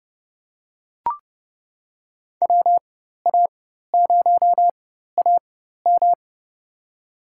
Morse code, heard as a keyed tone, about 30 wpm, spaced as for 14 wpm, 700 Hz. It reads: WA0AM
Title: USA Call Signs - 5 Characters - 4x Character Spacing - 30wpm